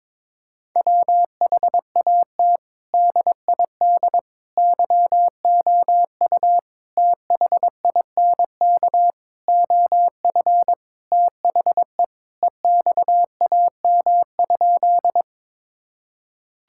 Morse code, heard as a keyed tone, about 22 wpm, 700 Hz. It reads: WHAT DID YOU THINK OF THE EXAM?
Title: Sentences from Top 1000 Words - Review Part 5 - 22wpm